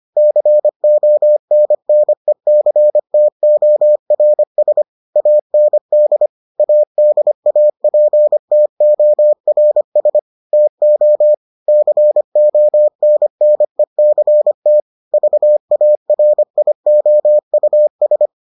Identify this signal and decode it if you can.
Morse code, heard as a keyed tone, about 25 wpm, 600 Hz. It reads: CONNECTORS AND ADAPTORS TO CONNECT VARIOUS